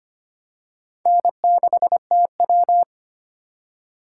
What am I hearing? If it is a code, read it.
Morse code, heard as a keyed tone, about 25 wpm, 700 Hz. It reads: N6TW
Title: K1USN SST QSO Practice - 25wpm